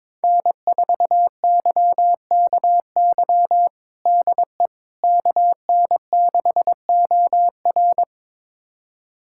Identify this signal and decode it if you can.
Morse code, heard as a keyed tone, about 22 wpm, 700 Hz. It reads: N4YKY DE KN6OR
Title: USA Callsign DE Callsign 22wpm